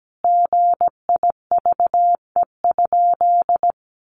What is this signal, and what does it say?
Morse code, heard as a keyed tone, about 17 wpm, 700 Hz. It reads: GIVE?
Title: Sentences from Top 1100 Words 17wpm